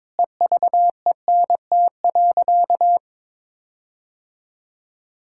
Morse code, heard as a keyed tone, about 22 wpm, 700 Hz. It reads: EVENT.
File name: Sentences with 7-Word Phrases - 22wpm